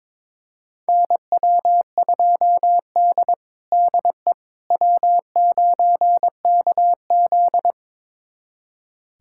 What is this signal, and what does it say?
Morse code, heard as a keyed tone, about 22 wpm, 700 Hz. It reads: NW2D DE W9KZ
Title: USA Callsign DE Callsign - 4 Characters - 22wpm